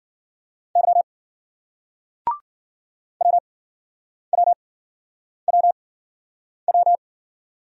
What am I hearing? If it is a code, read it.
Morse code, heard as a keyed tone, about 53 wpm, spaced as for 9 wpm, 700 Hz. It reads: Y WWWW